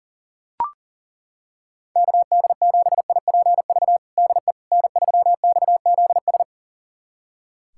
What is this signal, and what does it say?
Morse code, heard as a keyed tone, about 40 wpm, 700 Hz. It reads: KD7IPV DE N3XZS
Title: USA Callsign DE Callsign 40wpm